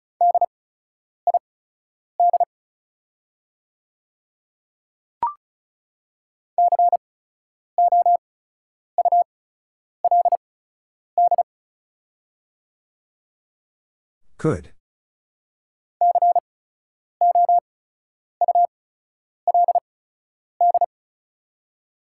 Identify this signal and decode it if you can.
Morse code, heard as a keyed tone, about 36 wpm, spaced as for 10 wpm, 700 Hz. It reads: DID COULD COULD
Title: 8x Character Spacing - Top 100 Words - 35wpm